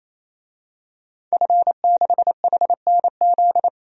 Morse code, heard as a keyed tone, about 28 wpm, 700 Hz. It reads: F6HNZ
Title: DXCC-100 Call Signs - 5 Characters - 28wpm